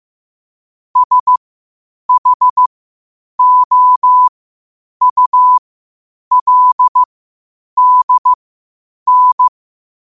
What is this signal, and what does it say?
Morse code, heard as a keyed tone, about 15 wpm, spaced as for 8 wpm, 1000 Hz. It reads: SHOULDN
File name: qotdfiles / 8wpm